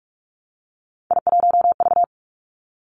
Morse code, heard as a keyed tone, about 46 wpm, 700 Hz. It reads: I1V
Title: Three Challenging Letter-Number Combinations 45wpm